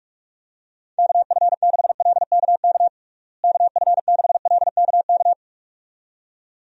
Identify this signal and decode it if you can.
Morse code, heard as a keyed tone, about 47 wpm, 700 Hz. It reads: KF6LKK KF6LKK
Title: USA Call Signs - 6 Characters - Repeated 2x - 45wpm